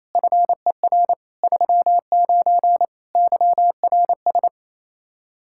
Morse code, heard as a keyed tone, about 27 wpm, 700 Hz. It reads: FER 39 YRS
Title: K1USN SST QSO Practice 28wpm